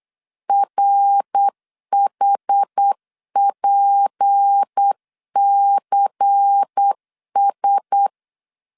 Morse code, heard as a keyed tone, about 8 wpm, 800 Hz. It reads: RHPCS
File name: LEZIONI / 6 PKX0 / 6 P 8wpm 3025